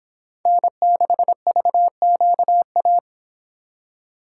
Morse code, heard as a keyed tone, about 26 wpm, 700 Hz. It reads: N6VQA